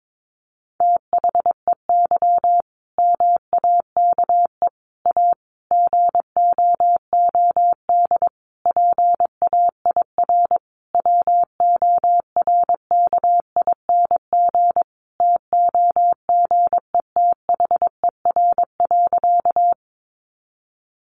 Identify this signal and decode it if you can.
Morse code, heard as a keyed tone, about 22 wpm, 700 Hz. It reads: THEY MAKE A GOOD PAIR WORKING TOGETHER.
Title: Sentences from Top 1200 Words 22wpm